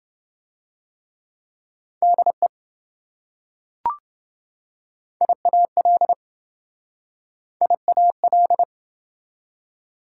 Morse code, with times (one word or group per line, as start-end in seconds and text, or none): 2.0-2.5 DE
5.2-6.1 IAL
7.6-8.6 IAL